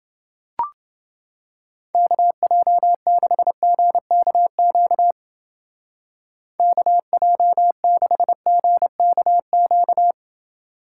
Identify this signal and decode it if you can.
Morse code, heard as a keyed tone, about 29 wpm, 700 Hz. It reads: KJ6GKQ KJ6GKQ